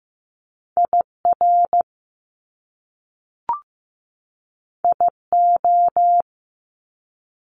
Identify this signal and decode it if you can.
Morse code, heard as a keyed tone, about 15 wpm, 700 Hz. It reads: IR IO
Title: Top Two Letter Combinations in Words 15wpm